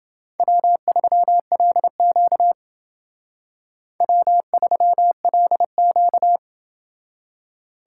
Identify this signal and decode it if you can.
Morse code, heard as a keyed tone, about 29 wpm, 700 Hz. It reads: W3LQ W3LQ